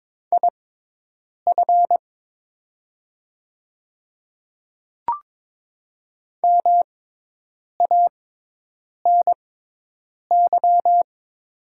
Morse code, heard as a keyed tone, about 22 wpm, spaced as for 8 wpm, 700 Hz. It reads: IF MANY
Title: 6x Character Spacing - Top 200 Words - 22wpm